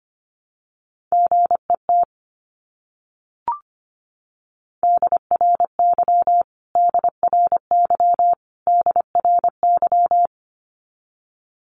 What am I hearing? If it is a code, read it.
Morse code, heard as a keyed tone, about 25 wpm, 700 Hz. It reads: GET DRY DRY DRY
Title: Top 500 Words Repeated 25wpm